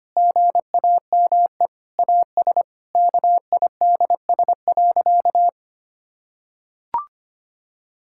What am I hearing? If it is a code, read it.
Morse code, heard as a keyed tone, about 25 wpm, 700 Hz. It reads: GAME AS KIDS.
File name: practice / 25wpm / Difficult Sentences with Hints - 25wpm